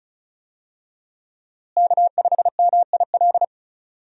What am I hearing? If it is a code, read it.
Morse code, heard as a keyed tone, about 34 wpm, 700 Hz. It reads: K5MIL
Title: USA Call Signs - 5 Characters - Repeated 2x - 35wpm